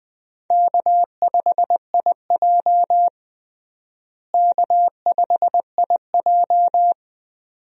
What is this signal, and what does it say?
Morse code, heard as a keyed tone, about 20 wpm, 700 Hz. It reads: K5IJ K5IJ
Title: USA Call Signs - 4 Characters - Repeated 2x - 20wpm